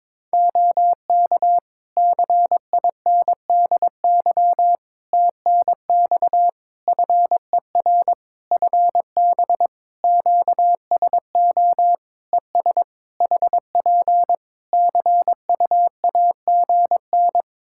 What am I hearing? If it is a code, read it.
Morse code, heard as a keyed tone, about 22 wpm, 700 Hz. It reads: OK CINDY TNX FER FB QSO ES HP CUAGN
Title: K1USN SST QSO Practice 22wpm